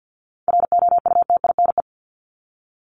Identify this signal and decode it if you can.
Morse code, heard as a keyed tone, about 50 wpm, 700 Hz. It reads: ROUTINE
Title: Top 1501-1750 Words 50wpm